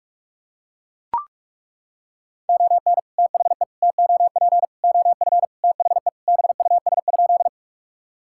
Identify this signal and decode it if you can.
Morse code, heard as a keyed tone, about 41 wpm, 700 Hz. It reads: ON THE TOP OF THE BUS?
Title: Sentences from Top 1000 Words - Review Part 6 - 45wpm